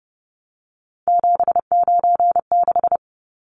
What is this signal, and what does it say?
Morse code, heard as a keyed tone, about 30 wpm, 700 Hz. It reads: 796
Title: Numbers - 3-Digit 30wpm